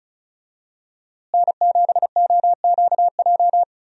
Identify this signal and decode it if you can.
Morse code, heard as a keyed tone, about 36 wpm, 700 Hz. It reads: N7OQJ
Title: USA Call Signs - 5 Characters - Repeated 3x - 35wpm